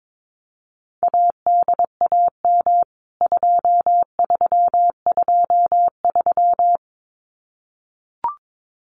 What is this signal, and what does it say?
Morse code, heard as a keyed tone, about 22 wpm, 700 Hz. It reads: ADAM 2323
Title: CWT Exchange 22wpm